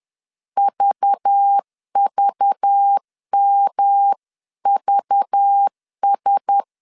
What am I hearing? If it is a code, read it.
Morse code, heard as a keyed tone, about 10 wpm, 800 Hz. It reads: VVMVS
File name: CW / DET / 1 6 1500C S B 10wpm 4137